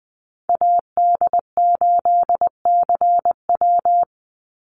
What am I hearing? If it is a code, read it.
Morse code, heard as a keyed tone, about 20 wpm, 700 Hz. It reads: AD8CW
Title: Call Signs - 20wpm